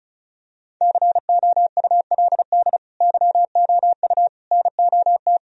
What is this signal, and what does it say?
Morse code, heard as a keyed tone, about 36 wpm, 700 Hz. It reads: COULD YOU NOT
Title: Sentences from Top 1000 Words - Review Part 1 - 35wpm